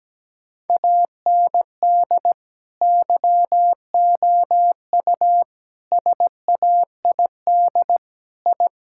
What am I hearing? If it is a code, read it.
Morse code, heard as a keyed tone, about 17 wpm, 700 Hz. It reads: AND YOU SAID I
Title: Sentences from Top 1000 Words - Review Part 5 - 17wpm